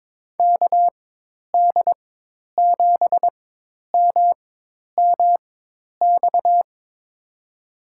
Morse code, heard as a keyed tone, about 22 wpm, spaced as for 10 wpm, 700 Hz. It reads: KD7MMX
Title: USA Call Signs - 6 Characters - 4x Character Spacing - 22wpm